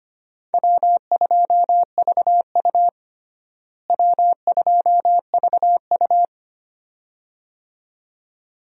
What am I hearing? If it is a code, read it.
Morse code, heard as a keyed tone, about 25 wpm, 700 Hz. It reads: W2VU W2VU
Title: USA Call Signs - 4 Characters - Repeated 2x - 25wpm